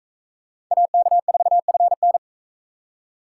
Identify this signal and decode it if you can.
Morse code, heard as a keyed tone, about 43 wpm, 700 Hz. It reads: AK4FN